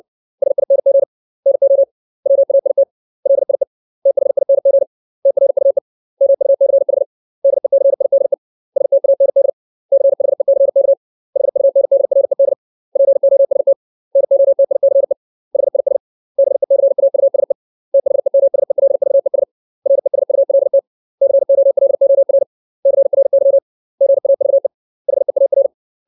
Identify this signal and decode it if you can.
Morse code, heard as a keyed tone, about 60 wpm, 550 Hz. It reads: SING NO WAIT LIE THING TRUE MUCH DECIDE STAND CHECK HUNDRED POST NOTICE HIS BECAUSE THEMSELVES RESULT COLOR CRY GAVE HEAR
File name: 632 words / 632 Words 60WPM